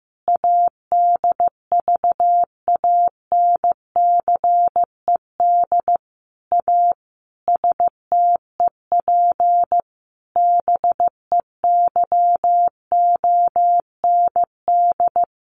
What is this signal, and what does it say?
Morse code, heard as a keyed tone, about 15 wpm, 700 Hz. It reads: ADVANCED A STEP BEYOND
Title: Sentences from Top 1200 Words 15wpm